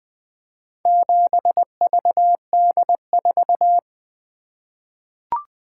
Morse code, heard as a keyed tone, about 20 wpm, 700 Hz. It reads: 7VD4
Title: Four Letter-Number Combinations - 20wpm